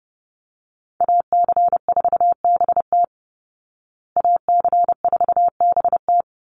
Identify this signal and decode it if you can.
Morse code, heard as a keyed tone, about 30 wpm, 700 Hz. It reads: AC4BT AC4BT
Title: Call Signs Repeated 3x - 30wpm